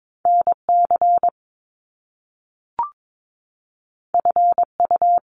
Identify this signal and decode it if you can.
Morse code, heard as a keyed tone, about 22 wpm, 700 Hz. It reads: NC FU
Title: Top Two Letter Combinations in Words 22wpm